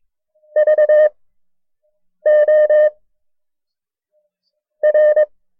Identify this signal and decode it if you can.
Morse code, heard as a keyed tone, about 22 wpm, 610 Hz. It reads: V O R
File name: FreeFiles / CW-practice / KB6MT-tape1-side2